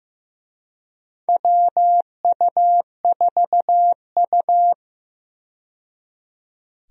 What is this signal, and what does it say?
Morse code, heard as a keyed tone, about 15 wpm, 700 Hz. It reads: WU4U
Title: USA Call Signs - 4 Characters - 15wpm